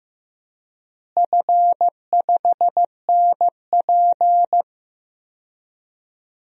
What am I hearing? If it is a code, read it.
Morse code, heard as a keyed tone, about 15 wpm, 700 Hz. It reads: F5NP
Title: DXCC-100 Call Signs - 4 Characters - Repeated 3x - 15wpm